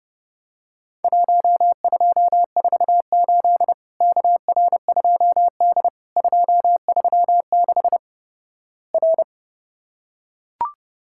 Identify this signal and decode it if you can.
Morse code, heard as a keyed tone, about 32 wpm, 700 Hz. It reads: 1248 KR2D 236 R
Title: Worked All Europe DX Contest - QTC Exchange - 30wpm